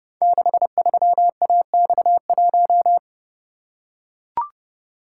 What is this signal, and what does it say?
Morse code, heard as a keyed tone, about 30 wpm, 700 Hz. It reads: 63AX1
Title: Code Groups - 5 Letter:Numbers - 30wpm